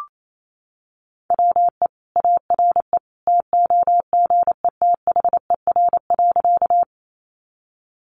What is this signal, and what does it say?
Morse code, heard as a keyed tone, about 28 wpm, 700 Hz. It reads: WE ARE TOGETHER.
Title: Sentences from Top 300 Words 28wpm